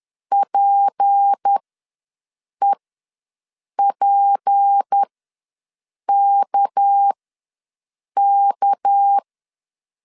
Morse code, heard as a keyed tone, about 10 wpm, spaced as for 6 wpm, 800 Hz. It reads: PEPKK
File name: LEZIONI / 6 PKX0 / 6 PK ws 10wpm 3745